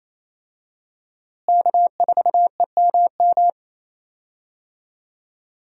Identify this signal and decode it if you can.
Morse code, heard as a keyed tone, about 27 wpm, 700 Hz. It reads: K4EMM
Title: USA Call Signs - 5 Characters - Repeated 2x - 28wpm